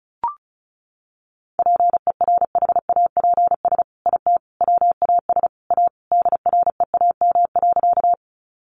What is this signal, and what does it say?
Morse code, heard as a keyed tone, about 34 wpm, 700 Hz. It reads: PERHAPS IT WAS A DREAM.
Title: Sentences from Top 1500 Words 35wpm